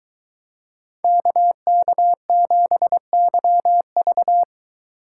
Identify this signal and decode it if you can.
Morse code, heard as a keyed tone, about 23 wpm, 700 Hz. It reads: KK7YV